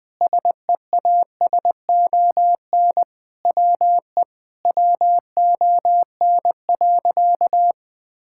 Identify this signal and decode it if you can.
Morse code, heard as a keyed tone, about 20 wpm, 700 Hz. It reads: SEASON WE WON.
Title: Sentences with 7-Word Phrases - 20wpm